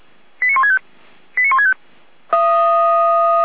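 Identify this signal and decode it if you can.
Five-tone selective call (ZVEI), heard as quick runs of five tones, tones of about 70 ms. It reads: 89165 89165
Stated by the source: ZVEI five-tone selective call received on an SDR.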